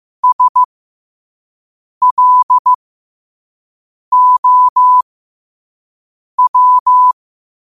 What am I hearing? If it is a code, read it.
Morse code, heard as a keyed tone, about 15 wpm, spaced as for 5 wpm, 1000 Hz. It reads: SLOW
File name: qotdfiles / 5wpm